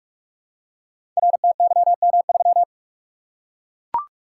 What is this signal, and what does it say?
Morse code, heard as a keyed tone, about 44 wpm, 700 Hz. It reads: RTYM3
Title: Code Groups - 5 Letter:Numbers - 45wpm